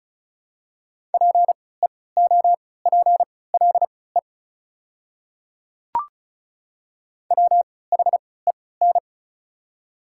Morse code, heard as a keyed tone, about 36 wpm, spaced as for 20 wpm, 700 Hz. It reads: PEOPLE WHEN
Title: 3x Character Spacing - Top 100 Words - 35wpm